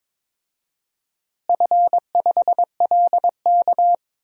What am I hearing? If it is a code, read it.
Morse code, heard as a keyed tone, about 22 wpm, 700 Hz. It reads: F5LK
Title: DXCC-100 Call Signs - 4 Characters - Repeated 3x - 22wpm